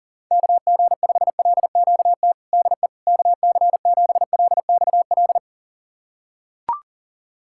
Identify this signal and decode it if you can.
Morse code, heard as a keyed tone, about 37 wpm, 700 Hz. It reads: KG5LQT DE KC7LXL
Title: USA Callsign DE Callsign - 6 Characters - 40wpm